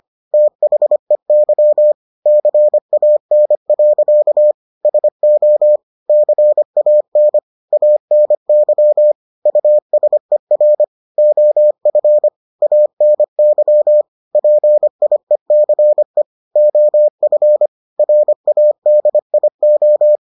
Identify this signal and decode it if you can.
Morse code, heard as a keyed tone, about 25 wpm, 600 Hz. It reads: THEY CAN. SO CAN ANY USER OF ANY PIECE OF RADIO